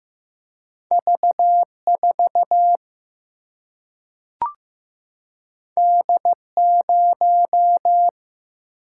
Morse code, heard as a keyed tone, about 15 wpm, 700 Hz. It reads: V 4 D 0
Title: Single-Letter Single-Number 15wpm